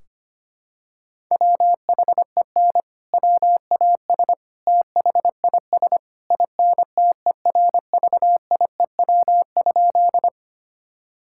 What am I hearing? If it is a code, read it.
Morse code, heard as a keyed tone, about 25 wpm, 700 Hz. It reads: WHEN WAS THIS INTERVIEW?
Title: Sentences from Top 1600 Words 25wpm